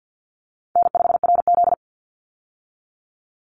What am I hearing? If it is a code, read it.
Morse code, heard as a keyed tone, about 48 wpm, 700 Hz. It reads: N5RZ
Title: Easy Call Signs Repeated 2x - 50wpm